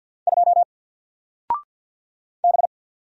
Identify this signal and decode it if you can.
Morse code, heard as a keyed tone, about 48 wpm, 700 Hz. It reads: 2 B
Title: Single Letter-Number 50wpm